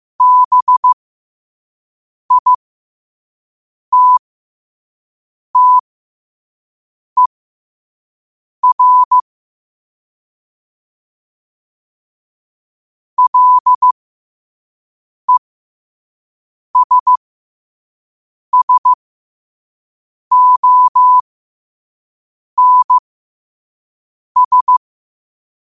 Morse code, heard as a keyed tone, about 15 wpm, spaced as for 5 wpm, 1000 Hz. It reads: BITTER LESSONS
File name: qotdfiles / 5wpm